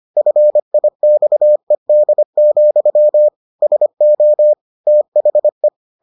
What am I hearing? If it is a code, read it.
Morse code, heard as a keyed tone, about 25 wpm, 600 Hz. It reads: FIXED, SO THE